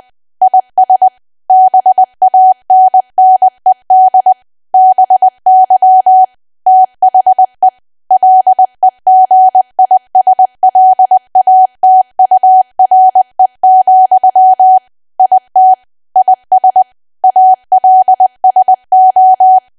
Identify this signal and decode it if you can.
Morse code, heard as a keyed tone, about 20 wpm, 750 Hz. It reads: IS BANNED BY THE LEGISLATURE, IT IS ALSO
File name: cwa / W1AW-Level3 / 20WPM